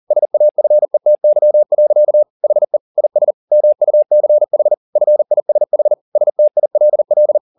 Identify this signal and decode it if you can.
Morse code, heard as a keyed tone, about 40 wpm, 600 Hz. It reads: SAFETY. HE IS MUCH FISH STILL
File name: the old man and the sea 2 40wpm